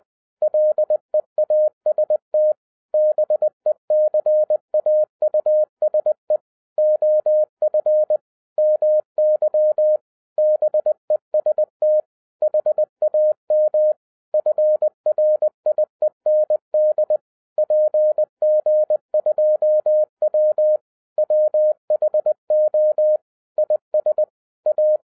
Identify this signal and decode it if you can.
Morse code, heard as a keyed tone, about 20 wpm, 610 Hz. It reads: LEAST BECAUSE OF MY BEST HAM FRIEND PG2W WHO IS A
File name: 2011-07-mp3 / n08 A LOT HAS CHANGED 20wpm